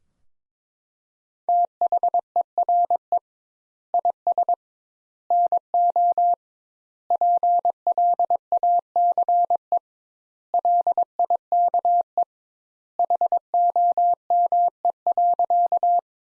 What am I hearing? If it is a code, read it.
Morse code, heard as a keyed tone, about 22 wpm, 700 Hz. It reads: THERE IS NO PLACE LIKE HOME.
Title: Sentences Top 200 Words - 22wpm